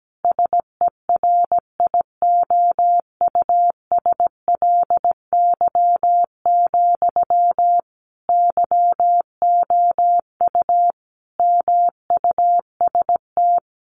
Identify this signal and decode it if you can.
Morse code, heard as a keyed tone, about 17 wpm, 700 Hz. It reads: SERIOUSLY, YOU MUST
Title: Sentences from Top 1200 Words 17wpm